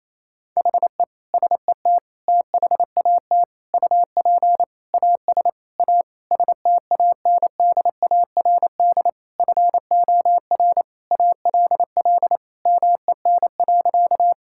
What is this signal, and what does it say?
Morse code, heard as a keyed tone, about 28 wpm, 700 Hz. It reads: HE SET THAT UP AS A STANDARD FOR ALL MEN.
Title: Sentences from Top 1000 Words - Review Part 6 - 28wpm